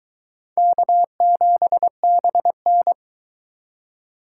Morse code, heard as a keyed tone, about 23 wpm, 700 Hz. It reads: K7BN